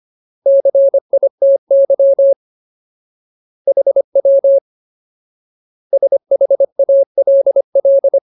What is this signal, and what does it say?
Morse code, heard as a keyed tone, about 25 wpm, 550 Hz. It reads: CITY HW SHALL
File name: Ultimate Word List / 25WPM Ultimate